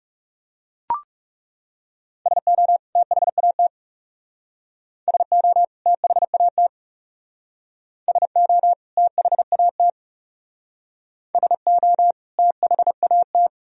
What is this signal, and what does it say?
Morse code, heard as a keyed tone, about 36 wpm, 700 Hz. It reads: SO THAT SO THAT SO THAT SO THAT
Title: Speed-Racing - Sets of 2 Words 30wpm